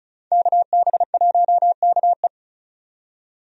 Morse code, heard as a keyed tone, about 34 wpm, 700 Hz. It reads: KB1KE
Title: USA Call Signs - 5 Characters - 35wpm